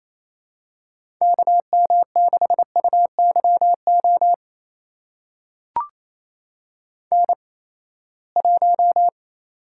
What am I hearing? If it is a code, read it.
Morse code, heard as a keyed tone, about 28 wpm, spaced as for 19 wpm, 700 Hz. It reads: KM6UYO N1
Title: USA Call Signs - 8x Character Spacing - 28wpm